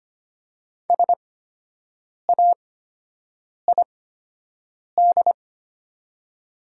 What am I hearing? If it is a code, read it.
Morse code, heard as a keyed tone, about 25 wpm, spaced as for 7 wpm, 700 Hz. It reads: SAID
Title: 8x Character Spacing - Top 100 Words - 25wpm